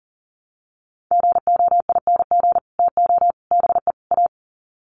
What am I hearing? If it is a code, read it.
Morse code, heard as a keyed tone, about 40 wpm, 700 Hz. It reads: GOING TO BE A
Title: Sets of 4 Words 40wpm